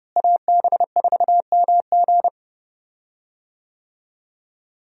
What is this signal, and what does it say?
Morse code, heard as a keyed tone, about 30 wpm, 700 Hz. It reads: AB4MG
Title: USA Call Signs - 5 Characters - Repeated 2x - 30wpm